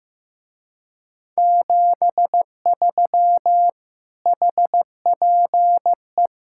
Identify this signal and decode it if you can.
Morse code, heard as a keyed tone, about 15 wpm, 700 Hz. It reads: 73 HPE
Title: Speed-Racing Common QSO Phrases Encore - 15wpm